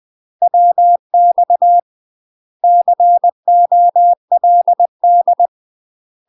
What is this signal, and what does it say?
Morse code, heard as a keyed tone, about 20 wpm, 700 Hz. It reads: WX COLD
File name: UK MASTER QSOs TRAINING / 20WPM / UK MASTER QSOs 0028